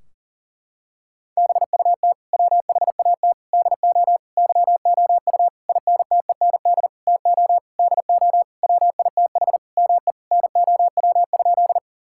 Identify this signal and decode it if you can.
Morse code, heard as a keyed tone, about 40 wpm, 700 Hz. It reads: BUT WHAT DO YOU INTEND TO DO WITH ME NOW?
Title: Sentences from Top 1600 Words 40wpm